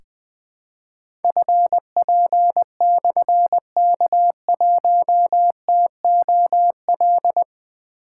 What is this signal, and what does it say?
Morse code, heard as a keyed tone, about 21 wpm, 700 Hz. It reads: FP/K1TOL
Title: Difficult Call Signs 20wpm